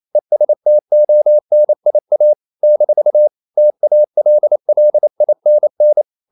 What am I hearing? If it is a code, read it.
Morse code, heard as a keyed tone, about 28 wpm, 600 Hz. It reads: ESTONIA - TALLINN
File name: CW / Practice / NationalCapitals28